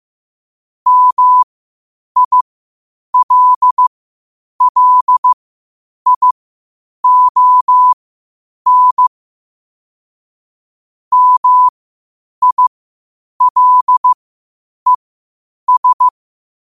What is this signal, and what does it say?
Morse code, heard as a keyed tone, about 15 wpm, spaced as for 8 wpm, 1000 Hz. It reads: MILLION MILES
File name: qotdfiles / 8wpm